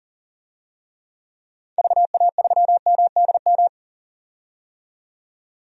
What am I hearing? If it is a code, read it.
Morse code, heard as a keyed tone, about 40 wpm, 700 Hz. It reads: VA3MDM
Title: DXCC-100 Call Signs - 6 Characters - 40wpm